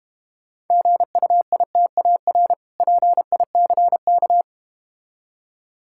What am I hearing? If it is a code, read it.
Morse code, heard as a keyed tone, about 32 wpm, 700 Hz. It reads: GUITAR PICK